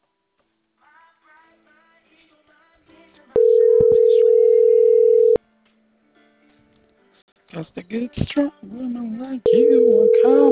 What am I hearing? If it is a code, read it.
Ringback tone (North America), 440 + 480 Hz, with speech.
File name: CENTRAL CAEA / out agcaea10 - .81383